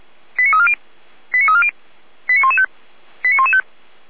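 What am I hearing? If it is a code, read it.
Five-tone selective call (ZVEI), heard as quick runs of five tones, tones of about 70 ms. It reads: 89340 89340 89115 89115
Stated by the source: ZVEI five-tone selective call 89174 received by SDR.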